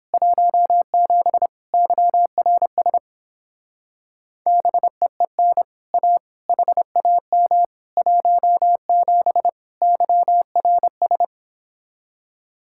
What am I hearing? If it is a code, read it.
Morse code, heard as a keyed tone, about 29 wpm, 700 Hz. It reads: 17 YRS BEEN A HAM 17 YRS